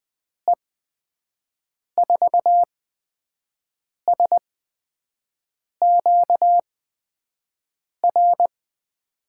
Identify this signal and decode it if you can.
Morse code, heard as a keyed tone, about 20 wpm, spaced as for 5 wpm, 700 Hz. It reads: E4SQR
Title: USA Call Signs - 6 Characters - 8x Character Spacing - 20wpm